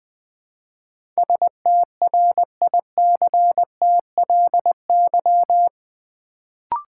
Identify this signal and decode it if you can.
Morse code, heard as a keyed tone, about 20 wpm, 700 Hz. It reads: STRICTLY
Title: Build A Word - 4 Letters At A Time - 8-Letter Words - 20wpm